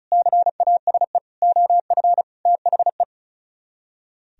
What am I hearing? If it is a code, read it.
Morse code, heard as a keyed tone, about 45 wpm, 700 Hz. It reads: CASE OF THE